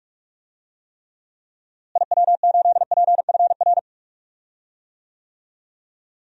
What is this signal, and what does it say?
Morse code, heard as a keyed tone, about 46 wpm, 700 Hz. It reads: IW8PFR
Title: DXCC-100 Call Signs - 6 Characters - Repeated 3x - 45wpm